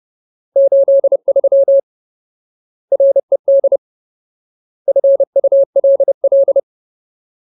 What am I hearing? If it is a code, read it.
Morse code, heard as a keyed tone, about 30 wpm, 550 Hz. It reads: 83 RED FULL
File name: Ultimate Word List / 30WPM Ultimate